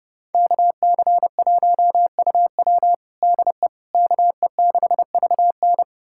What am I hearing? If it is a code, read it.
Morse code, heard as a keyed tone, about 30 wpm, 700 Hz. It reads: KC1UW DE KE6VN
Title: USA Callsign DE Callsign - 5 Characters - 30wpm